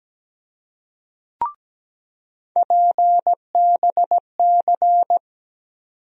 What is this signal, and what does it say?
Morse code, heard as a keyed tone, about 17 wpm, 700 Hz. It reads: PBC
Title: USA Call Sign Postfixes - 3 Letters 17wpm